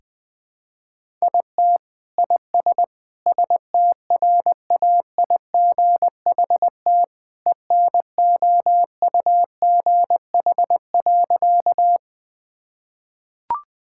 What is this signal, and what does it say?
Morse code, heard as a keyed tone, about 21 wpm, 700 Hz. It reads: IT IS STRAIGHT ENOUGH.
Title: Sentences from Top 1000 Words - Review Part 6 - 20wpm